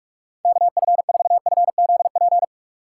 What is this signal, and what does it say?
Morse code, heard as a keyed tone, about 53 wpm, 700 Hz. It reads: KF4FZP
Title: Speed Racing - USA 2x3 Call Signs 45wpm